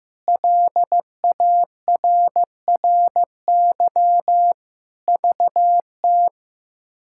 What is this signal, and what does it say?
Morse code, heard as a keyed tone, about 15 wpm, 700 Hz. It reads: LARRY VT
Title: K1USN SST QSO Practice - 15wpm